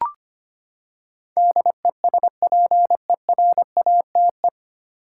Morse code, heard as a keyed tone, about 25 wpm, 700 Hz. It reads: DESPERATE
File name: Nine Letter Words - 25wpm